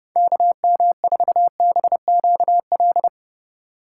Morse code, heard as a keyed tone, about 29 wpm, 700 Hz. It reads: KM4BQL